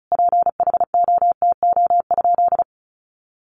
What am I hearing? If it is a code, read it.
Morse code, heard as a keyed tone, about 34 wpm, 700 Hz. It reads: PHOTO?
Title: Sentences from Top 1200 Words 35wpm